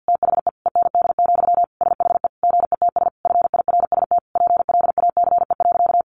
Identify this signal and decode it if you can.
Morse code, heard as a keyed tone, about 50 wpm, 700 Hz. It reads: THE END, SHE GETS FIRST PLACE.
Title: Additional Easy Phrases 50wpm